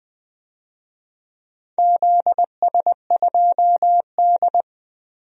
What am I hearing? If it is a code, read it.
Morse code, heard as a keyed tone, about 20 wpm, 700 Hz. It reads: ZS2D
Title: DXCC-100 Call Signs - 4 Characters - 20wpm